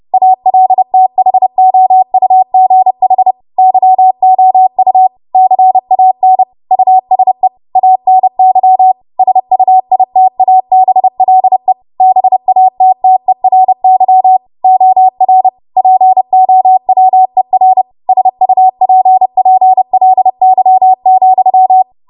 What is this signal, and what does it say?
Morse code, heard as a keyed tone, about 30 wpm, 750 Hz. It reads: ALTHOUGH YOU CAN USE ANY SUITABLE BATTERY OR POWER SUPPLY,